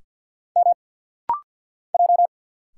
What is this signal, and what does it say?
Morse code, heard as a keyed tone, about 50 wpm, 700 Hz. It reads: M J
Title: Single Letters - Rapid-Fire 50wpm